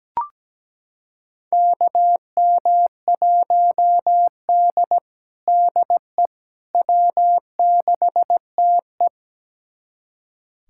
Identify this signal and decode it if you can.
Morse code, heard as a keyed tone, about 17 wpm, 700 Hz. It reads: KM1D DE W6TE
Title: USA Callsign DE Callsign - 4 Characters - 17wpm